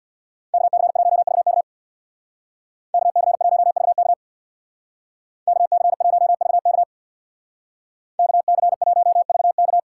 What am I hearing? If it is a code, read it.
Morse code, heard as a keyed tone, about 58 wpm, 700 Hz. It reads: KC1VK KC1VK KC1VK KC1VK
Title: Speed Racing - USA 2x2 Call Signs 50wpm